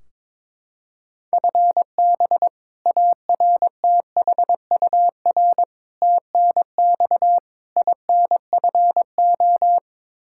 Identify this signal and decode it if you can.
Morse code, heard as a keyed tone, about 22 wpm, 700 Hz. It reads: FB ARTHUR TNX INFO
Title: Practice QSO - Exchange 3 - More Info - 22wpm